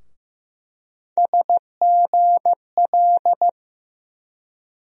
Morse code, heard as a keyed tone, about 15 wpm, 700 Hz. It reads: SGL
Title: USA Call Sign Postfixes - 3 Letters 15wpm